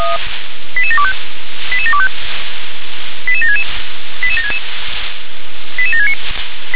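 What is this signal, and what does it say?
Five-tone selective call (ZVEI), heard as quick runs of five tones, tones of about 70 ms. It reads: 88725 88725 88766 88766 88769
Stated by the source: ZVEI five-tone selective call received with an SDR.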